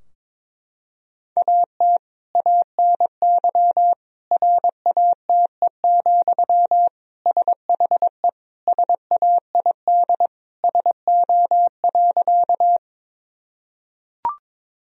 Morse code, heard as a keyed tone, about 22 wpm, 700 Hz. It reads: AT ANY RATE, SHE SAID SO.
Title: Sentences from Top 1000 Words - Review Part 4 - 22wpm